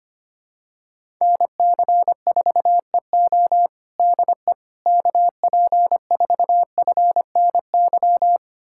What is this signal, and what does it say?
Morse code, heard as a keyed tone, about 25 wpm, 700 Hz. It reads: NC4EO DE KP4FNY
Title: USA Callsign DE Callsign 25wpm